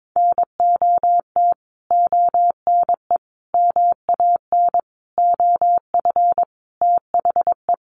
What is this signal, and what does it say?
Morse code, heard as a keyed tone, about 22 wpm, 700 Hz. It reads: NOT ONE MAN OF THE
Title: Sentences from Top 1500 Words 22wpm